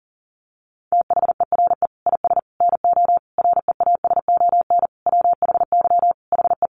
Morse code, heard as a keyed tone, about 40 wpm, 700 Hz. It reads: THERE IS NO REASON WHY HE
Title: Sentences from Top 700 Words 40wpm